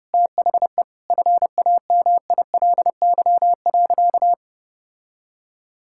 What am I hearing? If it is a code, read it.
Morse code, heard as a keyed tone, about 32 wpm, 700 Hz. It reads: THE FAMILY.
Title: Sentences from Top 1000 Words - Review Part 6 - 30wpm